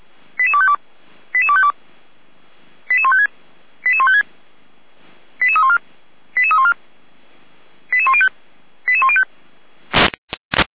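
Five-tone selective call (ZVEI), heard as quick runs of five tones, tones of about 70 ms. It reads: 88252 88252 88156 88156 88314 88314 88105 88105
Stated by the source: ZVEI five-tone selective call 88102 received by SDR.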